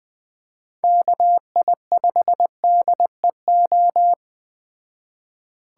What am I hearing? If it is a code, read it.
Morse code, heard as a keyed tone, about 20 wpm, 700 Hz. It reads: KI5DEO
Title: Speed Racing - USA 2x3 Call Signs 17wpm